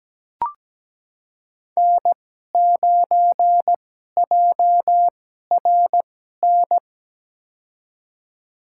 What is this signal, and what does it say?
Morse code, heard as a keyed tone, about 17 wpm, spaced as for 12 wpm, 700 Hz. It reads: N9JRN
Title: USA Call Signs - 5 Characters - 2x Character Spacing - 17wpm